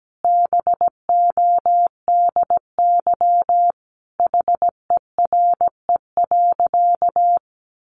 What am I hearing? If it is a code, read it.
Morse code, heard as a keyed tone, about 17 wpm, 700 Hz. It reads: BODY HERE.
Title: Sentences from Top 1100 Words 17wpm